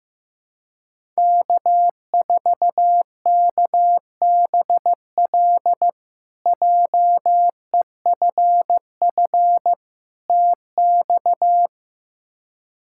Morse code, heard as a keyed tone, about 15 wpm, 700 Hz. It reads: K4KBL JEFF TX
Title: K1USN SST QSO Practice - 15wpm